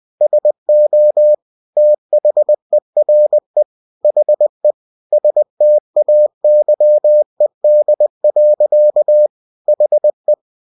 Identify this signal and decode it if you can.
Morse code, heard as a keyed tone, about 20 wpm, 600 Hz. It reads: SO THERE HE STAYED. HE
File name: cw-books / winnie-the-pooh / winnie-20wpm-page-0007